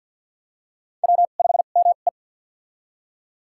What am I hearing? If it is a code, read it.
Morse code, heard as a keyed tone, about 51 wpm, spaced as for 36 wpm, 700 Hz. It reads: W5ME
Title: USA Call Signs - 4 Characters - 2x Character Spacing - 50wpm